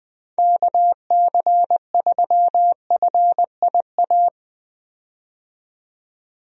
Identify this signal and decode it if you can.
Morse code, heard as a keyed tone, about 20 wpm, 700 Hz. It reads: KC3FIA
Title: USA Call Signs - 6 Characters - 20wpm